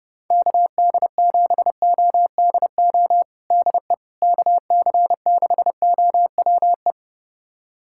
Morse code, heard as a keyed tone, about 30 wpm, 700 Hz. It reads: KD7ODO DE KC6OWE
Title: USA Callsign DE Callsign - 6 Characters - 30wpm